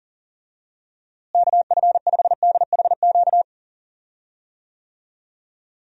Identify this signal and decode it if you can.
Morse code, heard as a keyed tone, about 40 wpm, 700 Hz. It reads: KF5DHQ
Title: USA Call Signs - 6 Characters - Repeated 3x - 40wpm